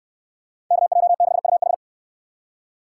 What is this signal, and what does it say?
Morse code, heard as a keyed tone, about 69 wpm, 700 Hz. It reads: KJ6UH